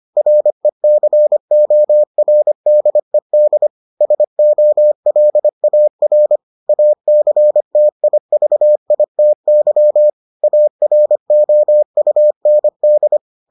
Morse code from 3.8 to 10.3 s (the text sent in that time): SOLAR ACTIVITY